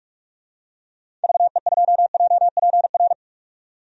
Morse code, heard as a keyed tone, about 42 wpm, 700 Hz. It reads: VE2JPR
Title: DXCC-100 Call Signs - 6 Characters - 45wpm